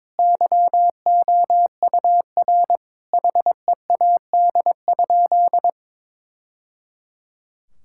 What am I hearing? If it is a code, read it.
Morse code, heard as a keyed tone, about 21 wpm, 700 Hz. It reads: YOUR HEAD?
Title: Sentences from Top 1000 Words - Review Part 5 - 22wpm